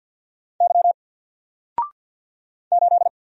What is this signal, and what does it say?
Morse code, heard as a keyed tone, about 48 wpm, 700 Hz. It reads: Y 8
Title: Single Character (Letter:Number:Punctuation:Prosign) 50wpm